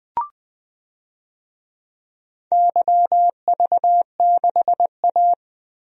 Morse code, heard as a keyed tone, about 20 wpm, 700 Hz. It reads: YV6A
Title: DXCC-100 Call Signs - 4 Characters - 20wpm